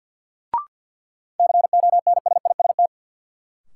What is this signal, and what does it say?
Morse code, heard as a keyed tone, about 49 wpm, 700 Hz. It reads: CONSIST
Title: Sentences from Top 1400 Words - New Word Drill Alt - 50wpm